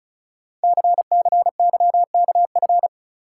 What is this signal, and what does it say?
Morse code, heard as a keyed tone, about 36 wpm, 700 Hz. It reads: CCYKF
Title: Code Groups - 5 Letters - 35wpm